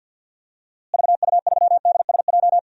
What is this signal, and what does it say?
Morse code, heard as a keyed tone, about 52 wpm, 700 Hz. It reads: VU3DSJ
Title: DXCC-100 Call Signs - 6 Characters - Repeated 3x - 50wpm